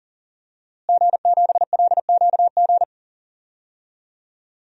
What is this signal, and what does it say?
Morse code, heard as a keyed tone, about 37 wpm, 700 Hz. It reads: G7LQG
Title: DXCC-100 Call Signs - 5 Characters - Repeated 2x - 40wpm